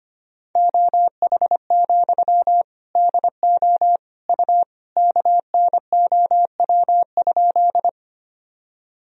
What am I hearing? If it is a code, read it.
Morse code, heard as a keyed tone, about 25 wpm, 700 Hz. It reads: OH, DO U KNOW?
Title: Sentences from Top 500 Words and Abbreviations 25wpm